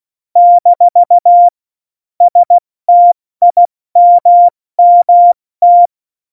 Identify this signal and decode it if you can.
Morse code, heard as a keyed tone, about 16 wpm, 700 Hz. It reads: - STIMMT
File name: Morsekurs / Witze / Witz 17 April 2018, 70 F80